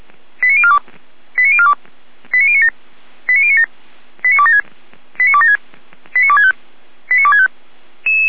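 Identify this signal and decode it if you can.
Five-tone selective call (ZVEI), heard as quick runs of five tones, tones of about 70 ms. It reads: 89042 89042 89097 89097 89276 89276 89265 89265
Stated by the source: ZVEI five-tone call received by SDR, on the feuerwehr-orlamuende.org server (fire brigade Orlamünde).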